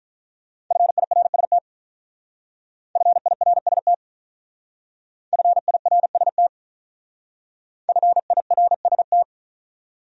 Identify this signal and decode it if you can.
Morse code, heard as a keyed tone, about 48 wpm, 700 Hz. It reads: FIRST FIRST FIRST FIRST